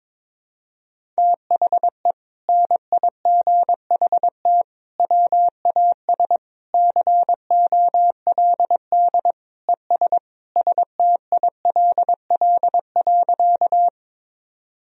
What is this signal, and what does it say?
Morse code, heard as a keyed tone, about 22 wpm, 700 Hz. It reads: THE NIGHT WAS COLD ES STILL.
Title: Sentences from Top 400 Words and Abbreviations 22wpm